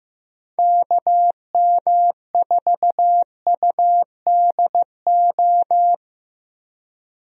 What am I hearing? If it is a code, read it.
Morse code, heard as a keyed tone, about 15 wpm, 700 Hz. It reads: KM4UDO
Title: USA Call Signs - 6 Characters - Repeated 2x - 15wpm